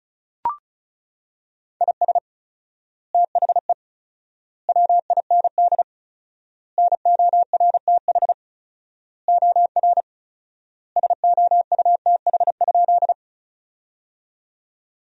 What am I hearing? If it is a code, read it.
Morse code, heard as a keyed tone, about 36 wpm, 700 Hz. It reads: IS THE WIND NORTH OR SOUTH?
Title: 4x Word-Spacing - Sentences from Top 400 Words - 35wpm